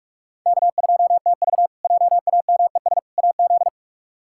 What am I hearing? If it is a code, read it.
Morse code, heard as a keyed tone, about 41 wpm, 700 Hz. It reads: K2TV JAMES AZ
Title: K1USN SST QSO Practice - 45wpm